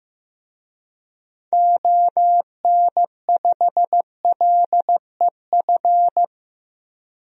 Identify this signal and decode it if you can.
Morse code, heard as a keyed tone, about 15 wpm, 700 Hz. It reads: ON5LEF
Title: DXCC-100 Call Signs - 6 Characters - 15wpm